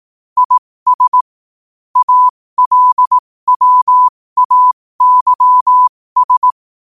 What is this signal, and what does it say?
Morse code, heard as a keyed tone, about 18 wpm, spaced as for 15 wpm, 1000 Hz. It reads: IS ALWAYS
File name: qotdfiles / 15wpm